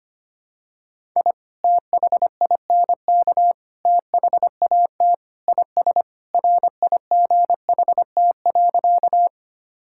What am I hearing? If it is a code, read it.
Morse code, heard as a keyed tone, about 25 wpm, 700 Hz. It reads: I THINK THAT IS RIGHT.
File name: practice / 25wpm / Sentences with 3-Word Phrases - 25wpm